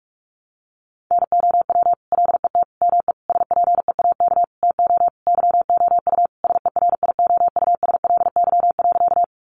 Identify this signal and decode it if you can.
Morse code, heard as a keyed tone, about 44 wpm, 700 Hz. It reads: NOW LET ME SPEAK TO YOU SERIOUSLY.
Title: Sentences from Top 1200 Words 45wpm